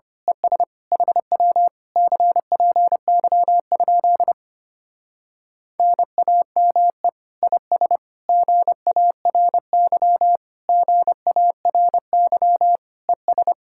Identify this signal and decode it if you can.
Morse code, heard as a keyed tone, about 25 wpm, 700 Hz. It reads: ES HW CPY? NAME IS GARY GARY ES
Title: Speed-Racing Common QSO Phrases - 25wpm